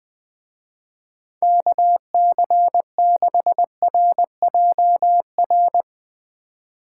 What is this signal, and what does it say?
Morse code, heard as a keyed tone, about 21 wpm, 700 Hz. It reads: KC6RJR
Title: Speed Racing - USA 2x3 Call Signs 20wpm